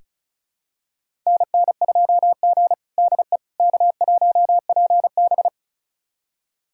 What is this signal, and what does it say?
Morse code, heard as a keyed tone, about 36 wpm, 700 Hz. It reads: NN2G DE K1PB
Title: USA Callsign DE Callsign - 4 Characters - 35wpm